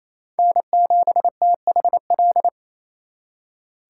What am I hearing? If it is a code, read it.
Morse code, heard as a keyed tone, about 36 wpm, 700 Hz. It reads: N7THL